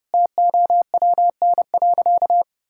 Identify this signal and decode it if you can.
Morse code, heard as a keyed tone, about 30 wpm, 700 Hz. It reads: TOWN.
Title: Sentences from Top 1000 Words - Review Part 6 - 30wpm